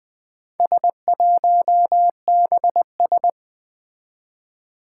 Morse code, heard as a keyed tone, about 20 wpm, 700 Hz. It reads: S1BS
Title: Four Challenging Letter-Number Combinations 20wpm